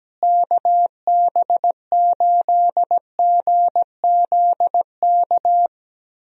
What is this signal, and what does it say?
Morse code, heard as a keyed tone, about 17 wpm, 700 Hz. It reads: KB8GZK
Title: USA Call Signs - 6 Characters - Repeated 2x - 17wpm